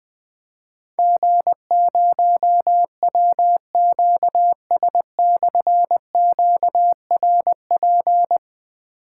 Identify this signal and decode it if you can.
Morse code, heard as a keyed tone, about 21 wpm, 700 Hz. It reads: G0WQS/QRP
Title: Difficult Call Signs 20wpm